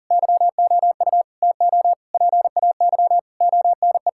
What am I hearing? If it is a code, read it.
Morse code, heard as a keyed tone, about 40 wpm, 700 Hz. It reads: YOU TO PAY ONE
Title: Sentences from Top 1000 Words - Review Part 6 - 40wpm